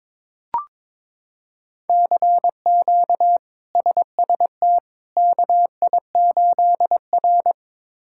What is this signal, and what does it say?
Morse code, heard as a keyed tone, about 21 wpm, 700 Hz. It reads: CQ SST KI8R
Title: K1USN SST QSO Practice - 22wpm